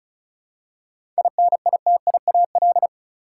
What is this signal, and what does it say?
Morse code, heard as a keyed tone, about 36 wpm, 700 Hz. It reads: INITIAL
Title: Seven Letter Words - 35wpm